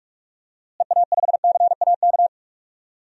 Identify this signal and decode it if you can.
Morse code, heard as a keyed tone, about 46 wpm, 700 Hz. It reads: EA5CAK
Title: DXCC-100 Call Signs - 6 Characters - Repeated 3x - 45wpm